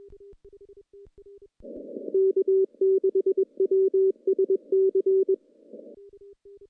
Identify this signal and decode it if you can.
Morse code, heard as a keyed tone, about 21 wpm, 380 Hz. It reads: K6WSC
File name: CW-SS-k5tr